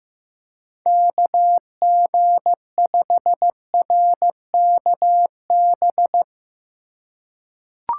Morse code, heard as a keyed tone, about 15 wpm, 700 Hz. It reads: KG5RKB
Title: USA Call Signs - 6 Characters - 3x Character Spacing - 15wpm